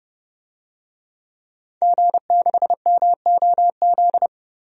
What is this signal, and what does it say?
Morse code, heard as a keyed tone, about 31 wpm, 690 Hz. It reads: G6MOZ
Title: DXCC-100 Call Signs - 5 Characters - 30wpm